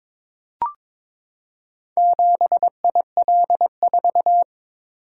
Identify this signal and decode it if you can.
Morse code, heard as a keyed tone, about 22 wpm, 700 Hz. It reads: 7IL4
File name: practice / 22wpm / Four Letter-Number Combinations - 22wpm